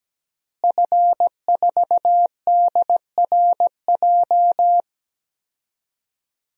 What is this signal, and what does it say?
Morse code, heard as a keyed tone, about 17 wpm, 700 Hz. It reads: F4DRJ
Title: DXCC-100 Call Signs - 5 Characters - Repeated 3x - 17wpm